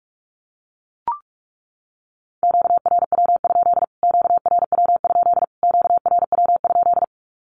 Morse code, heard as a keyed tone, about 44 wpm, 700 Hz. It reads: QRW? QRW? QRW?
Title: Q-Codes Repeated 3x 45wpm